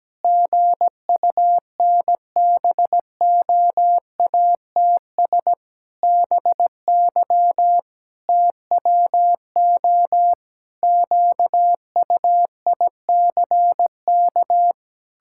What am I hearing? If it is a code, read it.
Morse code, heard as a keyed tone, about 17 wpm, 700 Hz. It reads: GUNBOATS BY TWO QUICK
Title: Pangram Challenge v2 - 17wpm